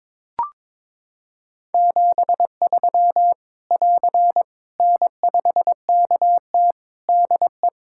Morse code, heard as a keyed tone, about 22 wpm, 700 Hz. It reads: 73 <AR> N5KT DE
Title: Practice QSO - Exchange 4 - Ending - 22wpm